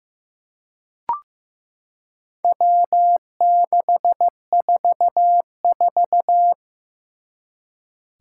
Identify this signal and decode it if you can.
Morse code, heard as a keyed tone, about 15 wpm, 700 Hz. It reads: W644
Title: Four Challenging Letter-Number Combinations 15wpm